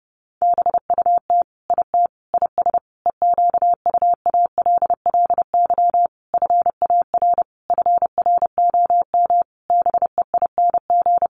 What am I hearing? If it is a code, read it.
Morse code, heard as a keyed tone, about 30 wpm, 700 Hz. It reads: BUT IT IS EQUALLY FAR FROM BEING
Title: Sentences from Top 1400 Words 30wpm